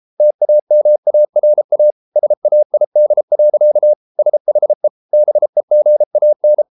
Morse code, heard as a keyed tone, about 33 wpm, 600 Hz. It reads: TAMARA SAID. SHE BEGAN